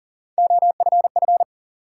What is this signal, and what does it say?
Morse code, heard as a keyed tone, about 40 wpm, 700 Hz. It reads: OFF
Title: Sets of 2 Words - New Word Drill - 40wpm